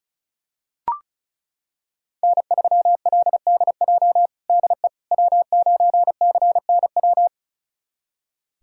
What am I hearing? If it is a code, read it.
Morse code, heard as a keyed tone, about 36 wpm, 700 Hz. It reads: N3LDJ DE W9CNW
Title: USA Callsign DE Callsign - 5 Characters - 35wpm